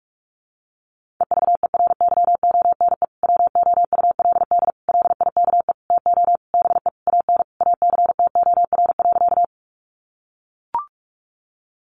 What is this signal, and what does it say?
Morse code, heard as a keyed tone, about 44 wpm, 700 Hz. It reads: EVERYONE WOULD LIKE TO BE AN ACTOR.
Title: Sentences from Top 1500 Words 45wpm